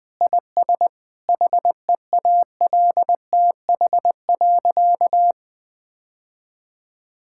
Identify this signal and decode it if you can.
Morse code, heard as a keyed tone, about 20 wpm, 700 Hz. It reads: IS HEALTH.
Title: Sentences with 3-Word Phrases - Spoken Beginning - 20wpm